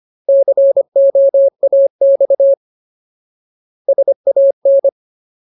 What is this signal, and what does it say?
Morse code, heard as a keyed tone, about 25 wpm, 550 Hz. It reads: COAX SAN
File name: Ultimate Word List / 25WPM Ultimate